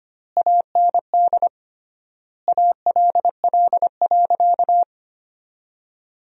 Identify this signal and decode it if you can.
Morse code, heard as a keyed tone, about 25 wpm, 700 Hz. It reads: AND ALL.
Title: 3x Word-Spacing - Sentences from Top 200 Words - 25wpm